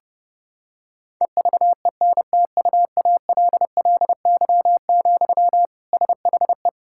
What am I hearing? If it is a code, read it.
Morse code, heard as a keyed tone, about 30 wpm, 700 Hz. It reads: EVENTUALLY, SHE
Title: Sentences from Top 1000 Words - Review Part 6 - 30wpm